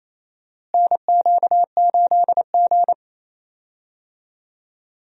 Morse code, heard as a keyed tone, about 28 wpm, 700 Hz. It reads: NQ8G
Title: USA Call Signs - 4 Characters - Repeated 2x - 28wpm